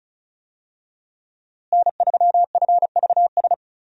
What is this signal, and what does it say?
Morse code, heard as a keyed tone, about 34 wpm, 700 Hz. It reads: N3FVS
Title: USA Call Signs - 5 Characters - Repeated 2x - 35wpm